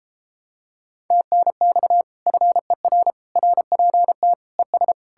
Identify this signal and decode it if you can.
Morse code, heard as a keyed tone, about 33 wpm, 700 Hz. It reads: TNX FER RPT ES